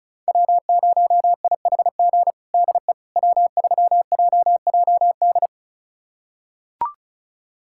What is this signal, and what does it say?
Morse code, heard as a keyed tone, about 36 wpm, 700 Hz. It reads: W0IHG DE W3JJD
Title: USA Callsign DE Callsign - 5 Characters - 35wpm